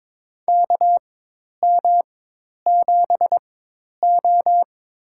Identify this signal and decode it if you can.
Morse code, heard as a keyed tone, about 23 wpm, spaced as for 10 wpm, 700 Hz. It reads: KM7O
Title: USA Call Signs - 4x Character Spacing - 22wpm